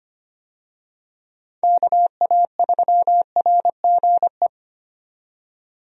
Morse code, heard as a keyed tone, about 25 wpm, 700 Hz. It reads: KA3RGE
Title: USA Call Signs - 6 Characters - 25wpm